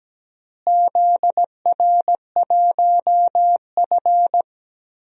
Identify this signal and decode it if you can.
Morse code, heard as a keyed tone, about 17 wpm, 700 Hz. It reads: ZR1F
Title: DXCC-100 Call Signs - 4 Characters - Repeated 2x - 17wpm